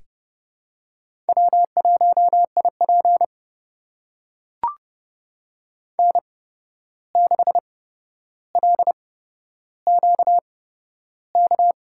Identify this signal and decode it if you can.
Morse code, heard as a keyed tone, about 30 wpm, spaced as for 12 wpm, 700 Hz. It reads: W1IP N6LQK
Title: USA Call Signs - 8x Character Spacing - 30wpm